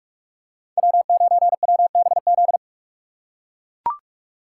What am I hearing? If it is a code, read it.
Morse code, heard as a keyed tone, about 41 wpm, 700 Hz. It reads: W9WBZ